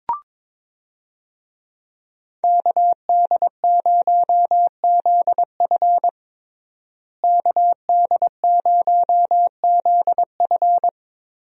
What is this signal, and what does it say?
Morse code, heard as a keyed tone, about 22 wpm, 700 Hz. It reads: KD0ZF KD0ZF
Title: USA Call Signs - 5 Characters - Repeated 3x - 22wpm